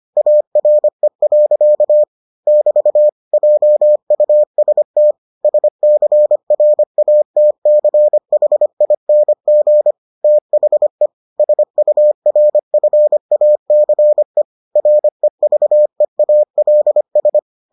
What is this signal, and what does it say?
Morse code, heard as a keyed tone, about 25 wpm, 600 Hz. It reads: ARE. <BT> JUST SCRATCHING THE SURFACE REVEALS